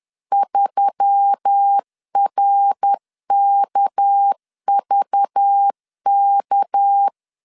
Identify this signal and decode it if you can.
Morse code, heard as a keyed tone, about 10 wpm, 800 Hz. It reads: 3RKVK
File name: CW / DET / 1 6 1500C S A 10wpm 4141